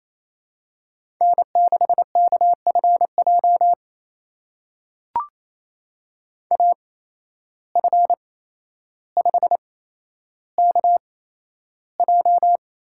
Morse code, heard as a keyed tone, about 28 wpm, spaced as for 12 wpm, 700 Hz. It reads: N6KFJ AF5KJ
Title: USA Call Signs - 5 Characters - 8x Character Spacing - 28wpm